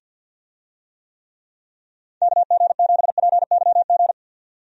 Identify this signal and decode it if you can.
Morse code, heard as a keyed tone, about 51 wpm, 700 Hz. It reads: KG7PYG
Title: USA Call Signs - 6 Characters - 50wpm